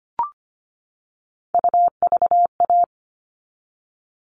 Morse code, heard as a keyed tone, about 25 wpm, 700 Hz. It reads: UVA
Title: All Cut Numbers - 3-Digit 25wpm